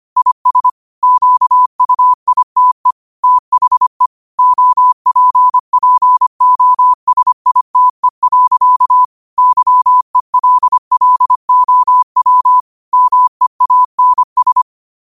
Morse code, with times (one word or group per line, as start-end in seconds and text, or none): 0.2-0.7 IS
1.0-2.9 QUITE
3.2-4.1 THE
4.4-9.0 OPPOSITE.
9.4-12.6 YELLOW
12.9-14.6 MEANS